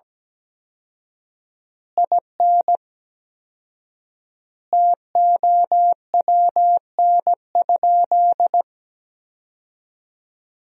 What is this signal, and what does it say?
Morse code, heard as a keyed tone, about 17 wpm, 700 Hz. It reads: IN TOWN?
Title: 4x Word-Spacing - Sentences from Top 500 Words - 17wpm